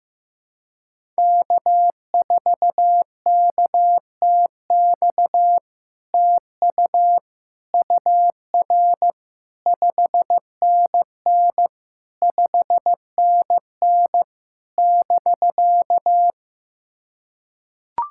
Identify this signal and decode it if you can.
Morse code, heard as a keyed tone, about 15 wpm, 700 Hz. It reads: K4KTX TU UR 5NN 5NN <BK>
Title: SOTA QSO Practice 15wpm